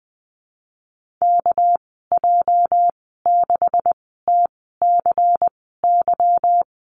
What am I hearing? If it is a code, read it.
Morse code, heard as a keyed tone, about 20 wpm, spaced as for 14 wpm, 700 Hz. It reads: KJ6TCY
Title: 213 Lesson 34g - Number 8 - Call Signs 20wpm